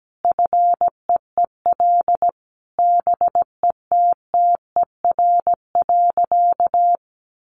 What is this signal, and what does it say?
Morse code, heard as a keyed tone, about 17 wpm, 700 Hz. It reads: FEEL BETTER.
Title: Easy Phrases 17wpm